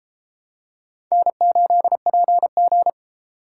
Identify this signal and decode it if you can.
Morse code, heard as a keyed tone, about 30 wpm, 700 Hz. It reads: N8PG